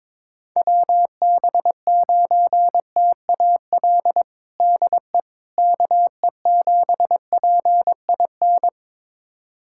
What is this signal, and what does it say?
Morse code, heard as a keyed tone, about 22 wpm, 700 Hz. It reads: WB9TAL DE KE7PIN
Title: USA Callsign DE Callsign - 6 Characters - 22wpm